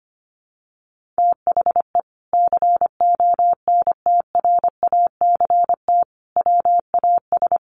Morse code, heard as a keyed tone, about 25 wpm, 700 Hz. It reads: THE CONTRACT WAS
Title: Sentences from Top 1400 Words 25wpm